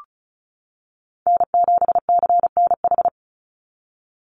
Morse code, heard as a keyed tone, about 34 wpm, 700 Hz. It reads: N7CNH
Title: NA SOTA Chaser Callsigns 35wpm